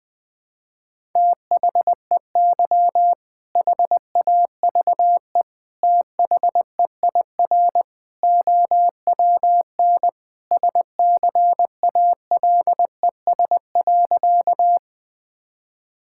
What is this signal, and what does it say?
Morse code, heard as a keyed tone, about 21 wpm, 700 Hz. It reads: THEY HAVE THEIR OWN SCALES.
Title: All-Age Appropriate Jokes - 20wpm